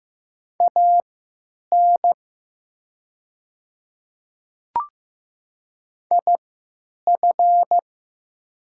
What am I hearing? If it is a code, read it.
Morse code, heard as a keyed tone, about 15 wpm, spaced as for 9 wpm, 700 Hz. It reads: AN IF
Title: 3x Character Spacing - Top 100 Words - 15wpm